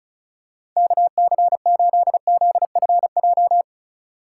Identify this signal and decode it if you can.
Morse code, heard as a keyed tone, about 36 wpm, 700 Hz. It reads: KC8ZFJ
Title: USA Call Signs - 6 Characters - 6x Character Spacing - 35wpm